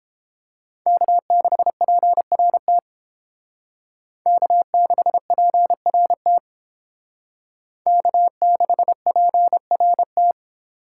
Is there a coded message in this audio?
Yes: Morse code from 0.9 to 10.3 s, heard as a keyed tone, about 31 wpm, 700 Hz. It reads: K6PRT K6PRT K6PRT